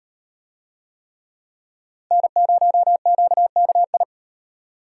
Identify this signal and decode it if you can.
Morse code, heard as a keyed tone, about 38 wpm, 700 Hz. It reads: N0QKI